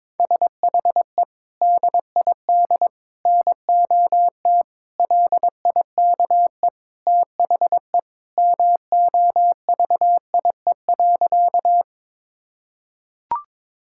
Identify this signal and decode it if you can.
Morse code, heard as a keyed tone, about 22 wpm, 700 Hz. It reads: SHE DID NOT LIKE THE MOVIE.
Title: Sentences from Top 1000 Words - Review Part 5 - 22wpm